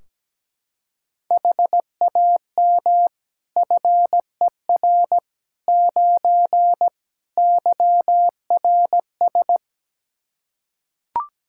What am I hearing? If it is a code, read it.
Morse code, heard as a keyed tone, about 17 wpm, 700 Hz. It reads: HAM FER 9 YRS
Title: K1USN SST QSO Practice 17wpm